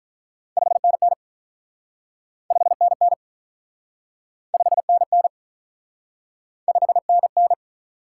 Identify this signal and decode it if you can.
Morse code, heard as a keyed tone, about 47 wpm, 700 Hz. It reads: 5NN 5NN 5NN 5NN